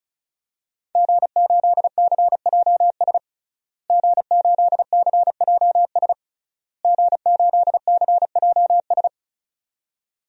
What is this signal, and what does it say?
Morse code, heard as a keyed tone, about 34 wpm, 700 Hz. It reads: G8CJS G8CJS G8CJS
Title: DXCC-100 Call Signs - 5 Characters - Repeated 3x - 35wpm